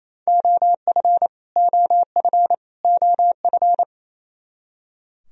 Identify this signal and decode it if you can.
Morse code, heard as a keyed tone, about 28 wpm, 700 Hz. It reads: OF OF OF
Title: Top 100 Words Repeated 28wpm